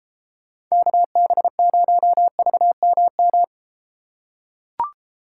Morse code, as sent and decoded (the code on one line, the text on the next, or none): -.- -... ----- ...- -- --
KB0VMM